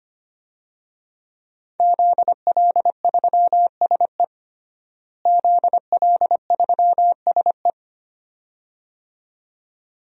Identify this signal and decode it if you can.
Morse code, heard as a keyed tone, about 25 wpm, 700 Hz. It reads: ZL3SE ZL3SE
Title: DXCC-100 Call Signs - 5 Characters - Repeated 2x - 25wpm